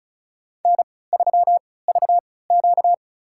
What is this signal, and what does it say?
Morse code, heard as a keyed tone, about 34 wpm, spaced as for 20 wpm, 700 Hz. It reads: N3VQ
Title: USA Call Signs - 4 Characters - 3x Character Spacing - 35wpm